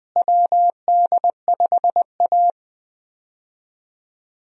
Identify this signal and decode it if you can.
Morse code, heard as a keyed tone, about 20 wpm, 700 Hz. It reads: WD5A
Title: USA Call Signs - 4 Characters - Repeated 3x - 20wpm